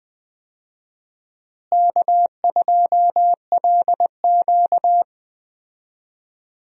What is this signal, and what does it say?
Morse code, heard as a keyed tone, about 20 wpm, 700 Hz. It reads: K2LQ
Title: USA Call Signs - 4 Characters - Repeated 3x - 20wpm